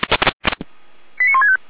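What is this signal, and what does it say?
Five-tone selective call (ZVEI), heard as quick runs of five tones, tones of about 70 ms. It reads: 89165
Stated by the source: ZVEI five-tone selective call 89163 received on SDR.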